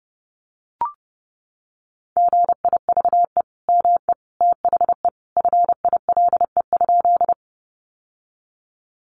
Morse code, heard as a keyed tone, about 30 wpm, 700 Hz. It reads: GIVE ME THE FILE?
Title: Sentences from Top 1200 Words 30wpm